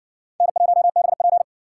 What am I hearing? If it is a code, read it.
Morse code, heard as a keyed tone, about 60 wpm, 700 Hz. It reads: N1BP